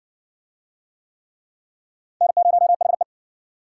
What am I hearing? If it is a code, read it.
Morse code, heard as a keyed tone, about 56 wpm, 700 Hz. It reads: N0HE